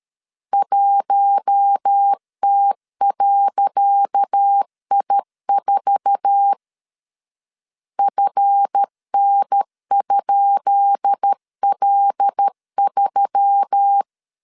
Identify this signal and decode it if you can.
Morse code, heard as a keyed tone, about 13 wpm, 800 Hz. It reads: 1T.I4 FN?L3
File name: LEZIONI / 10 SIGNS / 10C DOT 12wpm 2451